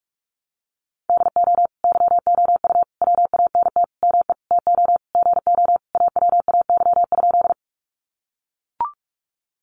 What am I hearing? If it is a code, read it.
Morse code, heard as a keyed tone, about 44 wpm, 700 Hz. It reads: DO YOU WANT ME TO GO AWAY?
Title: Sentences from Top 600 Words 45wpm